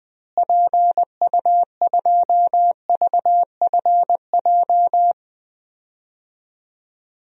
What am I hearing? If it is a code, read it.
Morse code, heard as a keyed tone, about 20 wpm, 700 Hz. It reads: PU2VFJ
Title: DXCC-100 Call Signs - 6 Characters - Repeated 2x - 20wpm